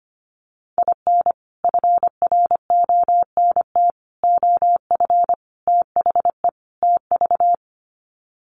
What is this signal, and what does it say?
Morse code, heard as a keyed tone, about 25 wpm, 700 Hz. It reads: IN FRONT OF THE TV
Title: Sets of 5 Words Encore2 25wpm